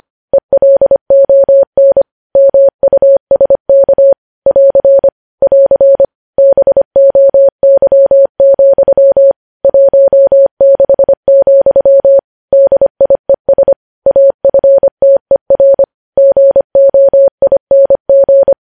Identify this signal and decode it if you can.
Morse code, heard as a keyed tone, about 25 wpm, 550 Hz. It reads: ELON MUSK <AR> <AR> BOY, 16, DIES AFTER GOING
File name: podcast / News headlines-1x-25x25